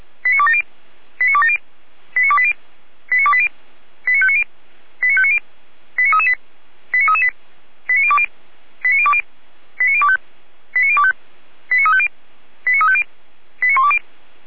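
Five-tone selective call (ZVEI), heard as quick runs of five tones, tones of about 70 ms. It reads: 89270 89270 89280 89280 89590 89590 89338 89338 89020 89020 89025 89025 89350 89350 89120
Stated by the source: ZVEI five-tone selective call received on SDR, from the feuerwehr-orlamuende fire brigade audio server.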